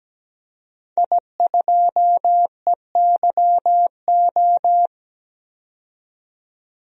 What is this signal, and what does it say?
Morse code, heard as a keyed tone, about 17 wpm, 700 Hz. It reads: I2EYO
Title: DXCC-100 Call Signs - 5 Characters - 17wpm